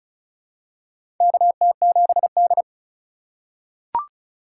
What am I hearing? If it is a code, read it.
Morse code, heard as a keyed tone, about 36 wpm, 700 Hz. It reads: KT7D
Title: USA Call Signs - 4x Character Spacing - 35wpm